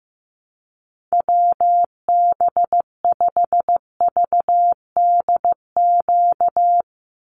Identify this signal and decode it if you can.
Morse code, heard as a keyed tone, about 15 wpm, 700 Hz. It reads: WB5VDQ
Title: USA Call Signs 15wpm